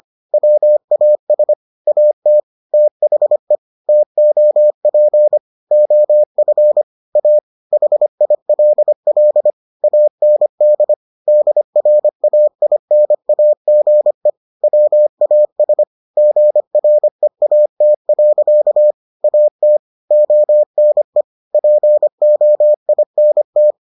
Morse code, heard as a keyed tone, about 25 wpm, 600 Hz. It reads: WAS AT THE TOP OF A HILL AND DRAINAGE WAS GREAT. AT ONE POINT